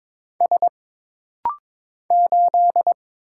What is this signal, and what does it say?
Morse code, heard as a keyed tone, about 22 wpm, 700 Hz. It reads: S 8
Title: Single Letter-Number - Rapid-Fire 22wpm